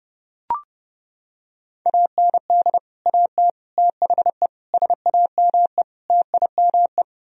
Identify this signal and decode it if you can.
Morse code, heard as a keyed tone, about 30 wpm, 700 Hz. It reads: AND AT THE SAME TIME